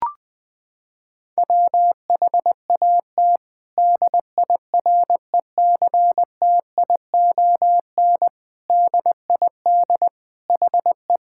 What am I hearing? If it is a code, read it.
Morse code, heard as a keyed tone, about 20 wpm, 700 Hz. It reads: WHAT DIRECTION DID HE
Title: Sentences from Top 1000 Words - Review Part 5 - 20wpm